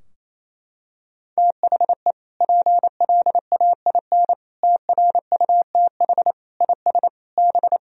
Morse code, heard as a keyed tone, about 28 wpm, 700 Hz. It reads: THE PLAIN TRUTH IS B
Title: Sentences from Top 1000 Words - Review Part 2 - 28wpm